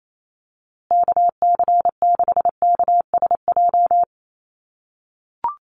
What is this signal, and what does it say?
Morse code, heard as a keyed tone, about 28 wpm, 700 Hz. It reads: KC6KSJ
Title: USA Call Signs Repeated 3x 28wpm